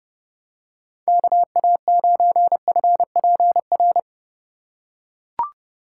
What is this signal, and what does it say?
Morse code, heard as a keyed tone, about 31 wpm, 700 Hz. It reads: KA9FPR
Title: USA Call Signs - 6 Characters - 2x Character Spacing - 30wpm